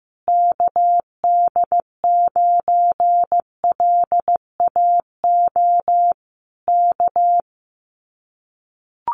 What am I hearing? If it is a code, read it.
Morse code, heard as a keyed tone, about 15 wpm, 700 Hz. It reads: KD9LAO K
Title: Common QSO Phrases Encore 15wpm